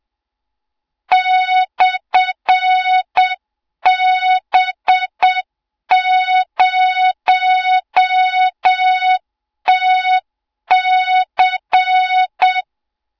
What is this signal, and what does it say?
Morse code, heard as a keyed tone, about 7 wpm, 3700 Hz, with noise. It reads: /B0TC
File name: Morze-HA2MN / hang / 20 GYAKORLAT